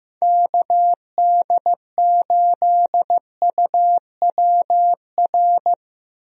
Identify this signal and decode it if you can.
Morse code, heard as a keyed tone, about 15 wpm, 700 Hz. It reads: KD8UWR
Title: USA Call Signs - 6 Characters - Repeated 2x - 15wpm